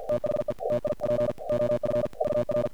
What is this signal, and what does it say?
Morse code, heard as a keyed tone, about 47 wpm, 600 Hz, with noise. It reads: THE TIP OF VR
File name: archived / CRSnet